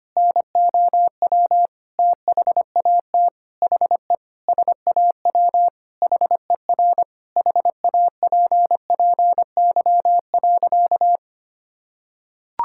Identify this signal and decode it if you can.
Morse code, heard as a keyed tone, about 25 wpm, 700 Hz. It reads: NOW THAT HE SAW HER HAPPY.
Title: Sentences from Top 1000 Words - Review Part 6 - 25wpm